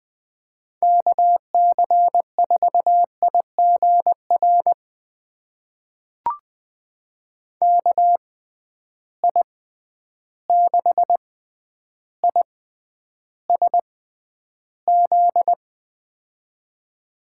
Morse code, heard as a keyed tone, about 20 wpm, spaced as for 10 wpm, 700 Hz. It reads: KC4IGR KI6ISZ
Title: USA Call Signs - 6 Characters - 6x Character Spacing - 20wpm